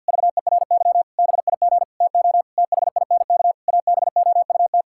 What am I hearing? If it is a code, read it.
Morse code, heard as a keyed tone, about 50 wpm, 700 Hz. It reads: VERY BIG TO THINK ABOUT
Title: Sentences from Top 1600 Words 50wpm